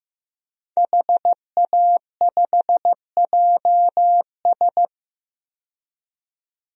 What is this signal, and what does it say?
Morse code, heard as a keyed tone, about 15 wpm, 700 Hz. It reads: HA5JS
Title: DXCC-100 Call Signs - 5 Characters - 15wpm